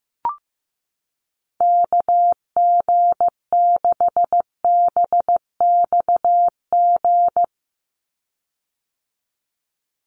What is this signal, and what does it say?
Morse code, heard as a keyed tone, about 15 wpm, 700 Hz. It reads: KG6BXG
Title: USA Call Signs 15wpm